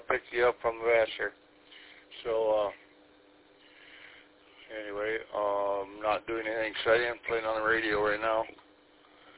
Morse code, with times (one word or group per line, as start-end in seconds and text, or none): none